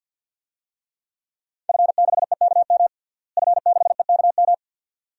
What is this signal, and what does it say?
Morse code, heard as a keyed tone, about 51 wpm, 700 Hz. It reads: F6EKM F6EKM
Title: DXCC-100 Call Signs - 5 Characters - Repeated 2x - 50wpm